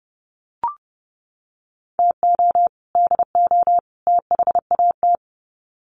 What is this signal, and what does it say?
Morse code, heard as a keyed tone, about 30 wpm, 700 Hz. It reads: TO DO THAT
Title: Sets of 3 Words 30wpm